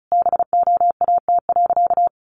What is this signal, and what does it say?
Morse code, heard as a keyed tone, about 34 wpm, 700 Hz. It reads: BOAT.
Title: Sentences from Top 1500 Words 35wpm